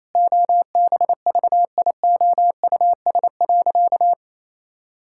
Morse code, heard as a keyed tone, about 28 wpm, 700 Hz. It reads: OBVIOUS.
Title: Sentences from Top 1600 Words 28wpm